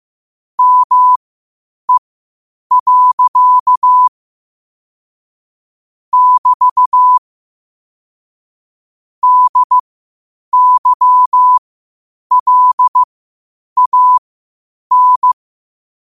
Morse code, heard as a keyed tone, about 15 wpm, spaced as for 8 wpm, 1000 Hz. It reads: ME. <BT> DYLAN
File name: qotdfiles / 8wpm